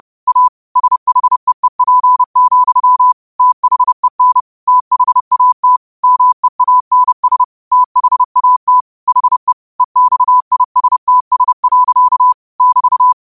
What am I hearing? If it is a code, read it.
Morse code, heard as a keyed tone, about 30 wpm, 1000 Hz. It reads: A SHEEP, THEN THAT MEANS THAT HE EXISTS. <BT>